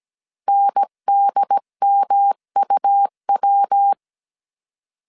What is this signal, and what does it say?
Morse code, heard as a keyed tone, about 17 wpm, 800 Hz. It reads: NDMUW